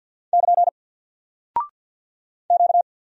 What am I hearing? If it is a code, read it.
Morse code, heard as a keyed tone, about 48 wpm, 700 Hz. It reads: <KN> Q
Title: Single Character (Letter:Number:Punctuation:Prosign) Alt 50wpm